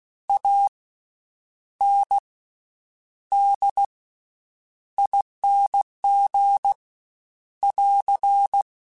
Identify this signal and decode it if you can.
Morse code, heard as a keyed tone, about 16 wpm, spaced as for 8 wpm, 800 Hz. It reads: ANDING <AR>
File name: Special INGs-mp3s / Special INGs-mp3s / Slow ing @15